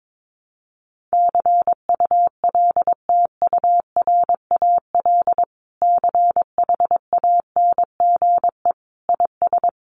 Morse code, heard as a keyed tone, about 22 wpm, 700 Hz. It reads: CULTURAL CHANGE IS
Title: Sentences from Top 1500 Words 22wpm